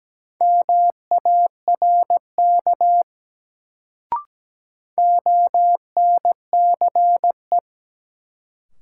Morse code, heard as a keyed tone, about 17 wpm, 700 Hz. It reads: MARK ONCE
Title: Sentences from Top 300 Words - New Word Drill - 17wpm